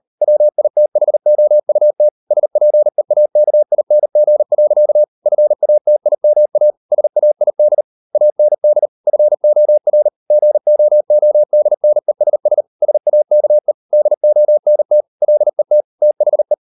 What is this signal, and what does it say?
Morse code, heard as a keyed tone, about 38 wpm, 600 Hz. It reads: WITHOUT SPEAKING. FATIMA SAID AND FOR GOODNESS SAKE DONT LET THE